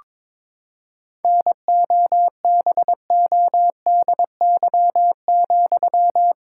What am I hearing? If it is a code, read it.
Morse code, heard as a keyed tone, about 22 wpm, 700 Hz. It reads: NOBODY,
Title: Sentences from Top 1600 Words 22wpm